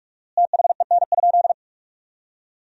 Morse code, heard as a keyed tone, about 46 wpm, 700 Hz. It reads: THEN?
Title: 4x Word-Spacing - Sentences from Top 100 Words - 45wpm